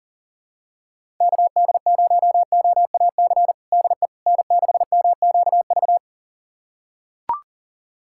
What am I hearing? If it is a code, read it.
Morse code, heard as a keyed tone, about 40 wpm, 700 Hz. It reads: KD0OAC DE N6MQV
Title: USA Callsign DE Callsign 40wpm